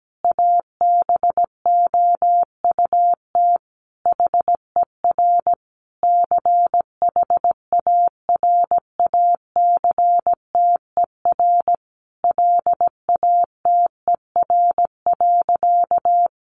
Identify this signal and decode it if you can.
Morse code, heard as a keyed tone, about 17 wpm, 700 Hz. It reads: ABOUT HER CHARACTER LATER.
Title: Sentences from Top 1200 Words 17wpm